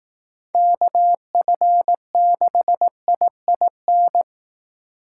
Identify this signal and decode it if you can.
Morse code, heard as a keyed tone, about 19 wpm, 700 Hz. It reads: KF6IIN
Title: Speed Racing - USA Call Signs 15wpm